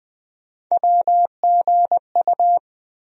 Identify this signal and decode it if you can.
Morse code, heard as a keyed tone, about 20 wpm, 700 Hz. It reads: WGU
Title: USA Call Sign Postfixes - 3 Letters 20wpm